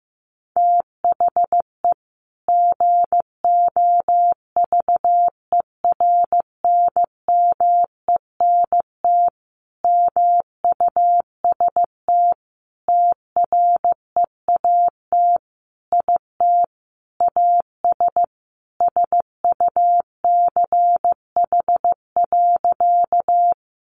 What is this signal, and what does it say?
Morse code, heard as a keyed tone, about 14 wpm, 700 Hz. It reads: THE GOVERNMENT MUST TREAT IT AS SUCH.
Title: Sentences from Top 1300 Words 15wpm